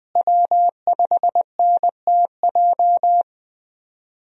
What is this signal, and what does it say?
Morse code, heard as a keyed tone, about 20 wpm, 700 Hz. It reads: W5NTJ
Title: USA Call Signs - 5 Characters - Repeated 3x - 20wpm